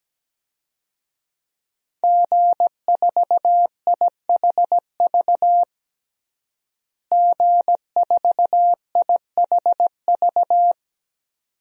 Morse code, heard as a keyed tone, about 17 wpm, 700 Hz. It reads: G4IHV G4IHV
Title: DXCC-100 Call Signs - 5 Characters - Repeated 2x - 17wpm